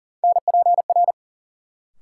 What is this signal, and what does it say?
Morse code, heard as a keyed tone, about 40 wpm, 700 Hz. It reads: NPR
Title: Sets of 5 Words - New Word Drill - 40wpm